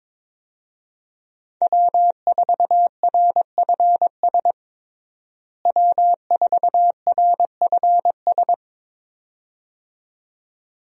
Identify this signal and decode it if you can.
Morse code, heard as a keyed tone, about 22 wpm, 700 Hz. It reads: W4RFS W4RFS
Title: USA Call Signs - 5 Characters - Repeated 2x - 22wpm